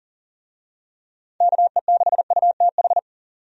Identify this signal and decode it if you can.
Morse code, heard as a keyed tone, about 40 wpm, 700 Hz. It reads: KE6UTH
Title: USA Call Signs - 6 Characters - 40wpm